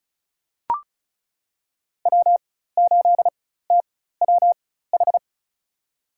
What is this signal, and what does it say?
Morse code, heard as a keyed tone, about 36 wpm, spaced as for 16 wpm, 700 Hz. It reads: W8TWH
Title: USA Call Signs - 5 Characters - 4x Character Spacing - 35wpm